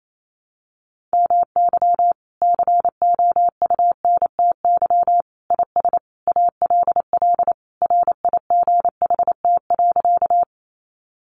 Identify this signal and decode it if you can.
Morse code, heard as a keyed tone, about 28 wpm, 700 Hz. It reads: MY COUNTY IS ALL RIGHT.
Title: Sentences from Top 1200 Words 28wpm